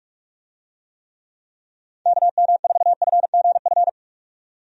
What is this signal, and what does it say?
Morse code, heard as a keyed tone, about 44 wpm, 700 Hz. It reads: KM4FGF
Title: USA Call Signs - 6 Characters - 45wpm